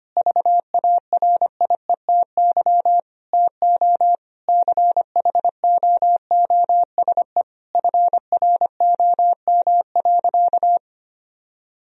Morse code, heard as a keyed tone, about 25 wpm, 700 Hz. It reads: VARIETY TO CHOOSE FROM.
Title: Sentences from Top 1000 Words - Review Part 4 - 25wpm